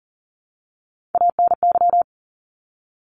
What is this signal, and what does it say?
Morse code, heard as a keyed tone, about 40 wpm, 700 Hz. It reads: ANY
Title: Top 300 Words 40wpm